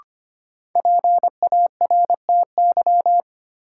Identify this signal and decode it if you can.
Morse code, heard as a keyed tone, about 25 wpm, 700 Hz. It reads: PARTY
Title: Sentences from Top 1400 Words - New Word Drill Alt - 25wpm